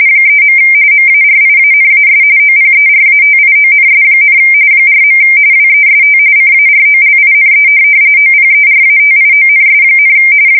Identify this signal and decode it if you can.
RTTY, about 45.45 baud. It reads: 3. MARIANAS OPERATIONS, 4. WEST CAROLINE OPERATIONS,
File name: fsk170 / USS-New-Jersey